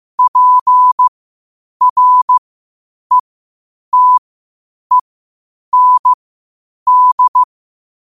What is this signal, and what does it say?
Morse code, heard as a keyed tone, about 15 wpm, spaced as for 8 wpm, 1000 Hz. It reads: PRETEND
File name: qotdfiles / 8wpm